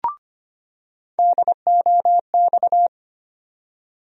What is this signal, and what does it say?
Morse code, heard as a keyed tone, about 25 wpm, 700 Hz. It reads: DOX
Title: USA Call Sign Postfixes - 3 Letters 25wpm